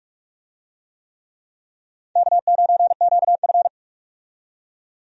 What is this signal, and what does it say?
Morse code, heard as a keyed tone, about 42 wpm, 700 Hz. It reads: K9QF
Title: USA Call Signs - 4 Characters - 45wpm